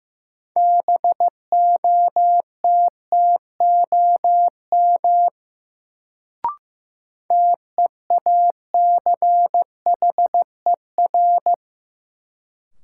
Morse code, heard as a keyed tone, about 15 wpm, 700 Hz. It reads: BOTTOM TEACHER
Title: Sentences from Top 900 Words - New Word Drill - 15wpm